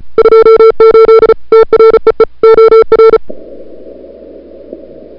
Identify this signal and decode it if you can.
Morse code, heard as a keyed tone, about 36 wpm, 440 Hz. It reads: 28 TREE OR
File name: CWSprint / February 2007 CW Sprint / 01-2007 CW Sprint Hour1